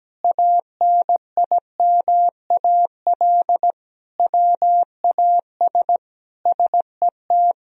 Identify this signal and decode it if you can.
Morse code, heard as a keyed tone, about 17 wpm, 700 Hz. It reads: ANIMAL WAS SET
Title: Sentences from Top 1000 Words - Review Part 1 - 17wpm